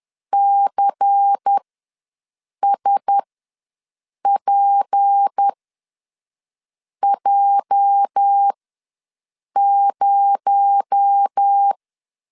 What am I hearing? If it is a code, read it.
Morse code, heard as a keyed tone, about 10 wpm, spaced as for 5 wpm, 800 Hz. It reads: CSPJ0
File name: LEZIONI / 8 YJ45 / 8 YJ ws 10wpm 4043